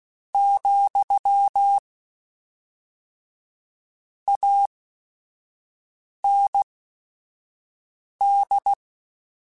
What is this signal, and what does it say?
Morse code, heard as a keyed tone, about 16 wpm, spaced as for 5 wpm, 800 Hz. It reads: , AND